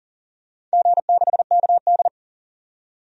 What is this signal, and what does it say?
Morse code, heard as a keyed tone, about 40 wpm, 690 Hz. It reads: G6KD
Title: DXCC-100 Call Signs - 4 Characters - 40wpm